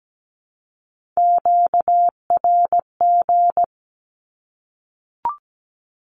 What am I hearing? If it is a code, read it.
Morse code, heard as a keyed tone, about 17 wpm, 700 Hz. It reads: QRG
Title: Q-Codes 17wpm